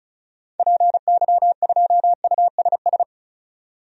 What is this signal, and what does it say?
Morse code, heard as a keyed tone, about 34 wpm, 700 Hz. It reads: PY2USS
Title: DXCC-100 Call Signs - 6 Characters - 35wpm